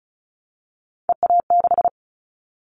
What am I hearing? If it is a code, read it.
Morse code, heard as a keyed tone, about 36 wpm, 700 Hz. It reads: EA6
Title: Callsign Prefixes for Top 100 DXCC - 35wpm